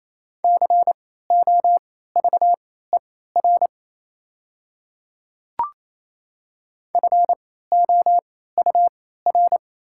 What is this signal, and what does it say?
Morse code, heard as a keyed tone, about 28 wpm, spaced as for 16 wpm, 700 Hz. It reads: COVER FOUR
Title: 3x Character Spacing - Top 300 Words - 28wpm